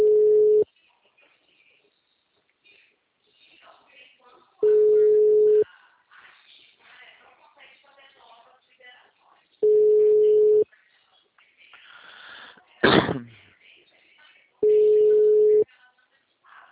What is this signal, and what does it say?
Ringback tone (Europe, 425 Hz), with speech.